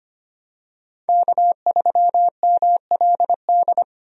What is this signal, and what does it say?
Morse code, heard as a keyed tone, about 25 wpm, 700 Hz. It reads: K3MLD
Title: USA Call Signs - 5 Characters - Repeated 2x - 25wpm